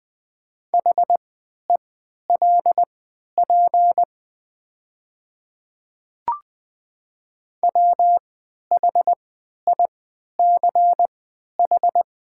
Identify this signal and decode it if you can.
Morse code, heard as a keyed tone, about 20 wpm, spaced as for 11 wpm, 700 Hz. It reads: HELP WHICH
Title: 3x Character Spacing - Top 200 Words - 20wpm